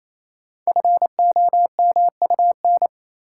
Morse code, heard as a keyed tone, about 27 wpm, 690 Hz. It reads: FOMUN
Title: Code Groups - 5 Letters - 28wpm